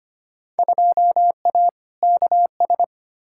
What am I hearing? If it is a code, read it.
Morse code, heard as a keyed tone, about 25 wpm, 700 Hz. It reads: 2A KS
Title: Field Day Exchanges - 25wpm